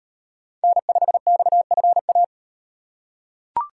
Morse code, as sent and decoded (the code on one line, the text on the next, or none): -. ..... -..- ..-. .-
N5XFA